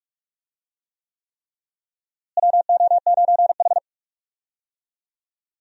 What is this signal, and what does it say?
Morse code, heard as a keyed tone, about 42 wpm, 700 Hz. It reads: WO9H
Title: USA Call Signs - 4 Characters - Repeated 3x - 45wpm